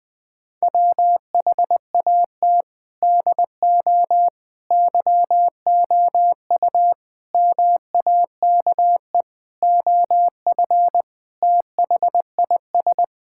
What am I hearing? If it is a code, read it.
Morse code, heard as a keyed tone, about 20 wpm, 700 Hz. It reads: WHAT DO YOU MAKE OF THIS
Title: Sentences with 5-Word Phrases - 20wpm